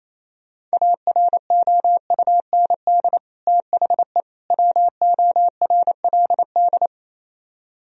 Morse code, heard as a keyed tone, about 37 wpm, 700 Hz. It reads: AROUND THE WORLD